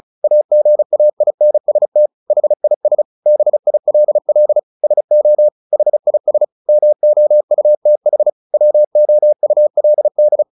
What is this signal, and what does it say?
Morse code, heard as a keyed tone, about 34 wpm, 600 Hz. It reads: AGAINST HIS BILL SO HIS MOUTH WOULD
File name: the old man and the sea 2 35wpm